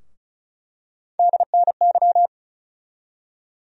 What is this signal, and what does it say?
Morse code, heard as a keyed tone, about 34 wpm, 700 Hz. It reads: DNY
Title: USA Call Sign Postfixes - 3 Letters 35wpm